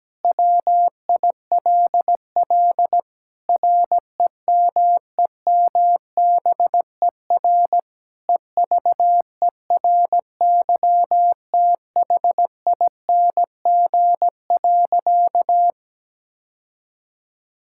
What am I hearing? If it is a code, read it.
Morse code, heard as a keyed tone, about 17 wpm, 700 Hz. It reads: WILL REMEMBER EVERYTHING.
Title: Sentences from Top 1000 Words - Review Part 5 - 17wpm